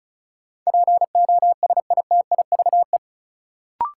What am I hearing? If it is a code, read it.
Morse code, heard as a keyed tone, about 34 wpm, 700 Hz. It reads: POSITIVE
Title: Sentences from Top 1000 Words - Review Part 5 - New Word Drill - 35wpm